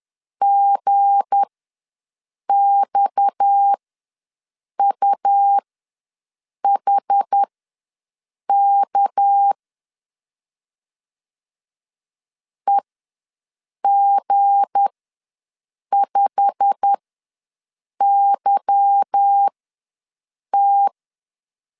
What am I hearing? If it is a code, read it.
Morse code, heard as a keyed tone, about 10 wpm, spaced as for 6 wpm, 800 Hz. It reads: GXUHK EG5YT
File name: LEZIONI / 8 YJ45 / 8 YJ45 ws 10wpm 3843